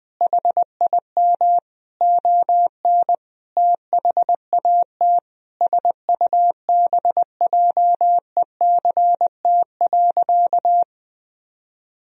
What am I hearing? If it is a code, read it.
Morse code, heard as a keyed tone, about 19 wpm, 700 Hz. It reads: HIM ON THAT SUBJECT.
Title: Sentences from Top 1000 Words - Review Part 4 - 20wpm